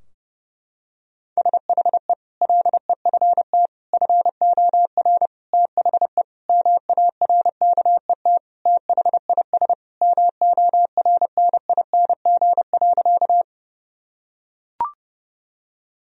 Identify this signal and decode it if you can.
Morse code, heard as a keyed tone, about 30 wpm, 700 Hz. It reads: SHE LEFT FOR THE MARKET THIS MORNING.
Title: Sentences from Top 1000 Words - Review Part 3 - 30wpm